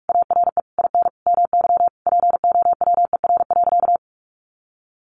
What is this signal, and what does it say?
Morse code, heard as a keyed tone, about 44 wpm, 700 Hz. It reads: ARE IN MY POWER.
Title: Sentences from Top 500 Words 45wpm